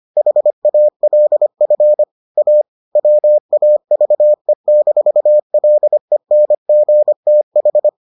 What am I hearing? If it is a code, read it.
Morse code, heard as a keyed tone, about 25 wpm, 600 Hz. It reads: HALF A WAVE-LENGTH